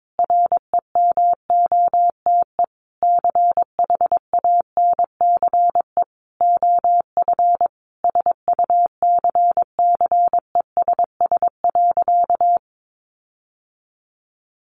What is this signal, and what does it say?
Morse code, heard as a keyed tone, about 22 wpm, 700 Hz. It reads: REMOTE CHANCE OF SUCCESS.
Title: Sentences from Top 1300 Words 22wpm